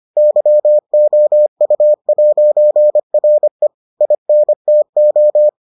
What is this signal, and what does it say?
Morse code, heard as a keyed tone, about 25 wpm, 600 Hz. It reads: YOU'RE INTO